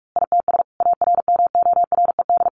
Morse code, heard as a keyed tone, about 44 wpm, 700 Hz. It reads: ITS ARMORED
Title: A Sound Of Thunder 45wpm